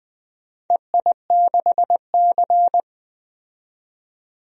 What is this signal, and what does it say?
Morse code, heard as a keyed tone, about 20 wpm, 700 Hz. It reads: EI6C
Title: DXCC-100 Call Signs - 4 Characters - 20wpm